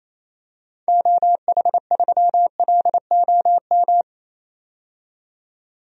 Morse code, heard as a keyed tone, about 28 wpm, 700 Hz. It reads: OH3LOM
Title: DXCC-100 Call Signs - 6 Characters - 28wpm